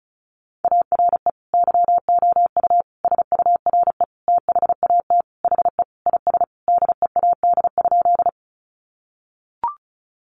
Morse code, heard as a keyed tone, about 34 wpm, 700 Hz. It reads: ARE YOU SURE THAT HE IS DEAD?
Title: Sentences from Top 1500 Words 35wpm